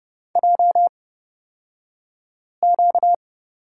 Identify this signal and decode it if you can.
Morse code, heard as a keyed tone, about 30 wpm, spaced as for 5 wpm, 700 Hz. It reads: JQ
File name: Lesson 38d - Letter q - Two Characters 30wpm